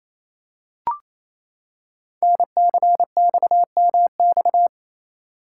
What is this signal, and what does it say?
Morse code, heard as a keyed tone, about 29 wpm, 700 Hz. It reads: NCXMX
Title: Code Groups - 5 Letters - 28wpm